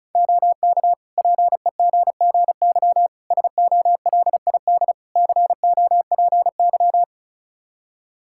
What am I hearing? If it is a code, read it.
Morse code, heard as a keyed tone, about 36 wpm, 700 Hz. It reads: OK PEGGY SOLID COPY
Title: Practice QSO - Exchange 3 - More Info - 35wpm